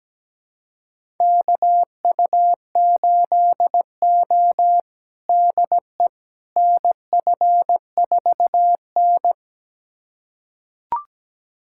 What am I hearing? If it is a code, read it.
Morse code, heard as a keyed tone, about 17 wpm, 700 Hz. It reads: KU8O DE NF4N
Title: USA Callsign DE Callsign - 4 Characters - 17wpm